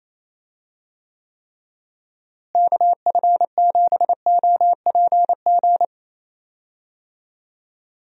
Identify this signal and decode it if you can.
Morse code, heard as a keyed tone, about 28 wpm, 700 Hz. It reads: KF7OPG
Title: USA Call Signs - 6 Characters - Repeated 3x - 28wpm